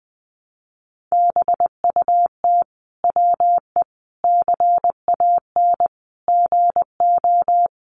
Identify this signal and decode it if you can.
Morse code, heard as a keyed tone, about 20 wpm, 700 Hz. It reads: BUT WE CAN GO
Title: Sentences from Top 100 Words 20wpm